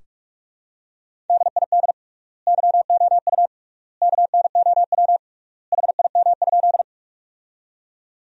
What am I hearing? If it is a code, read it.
Morse code, heard as a keyed tone, about 44 wpm, 700 Hz. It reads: DID YOU KNOW HIM?
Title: 3x Word-Spacing - Sentences from Top 100 Words - 45wpm